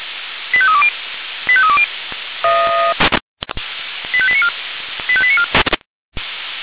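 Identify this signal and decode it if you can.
Five-tone selective call (ZVEI), heard as quick runs of five tones, tones of about 70 ms. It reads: 85320 85320 85894 85894
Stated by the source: ZVEI five-tone selective call received by SDR.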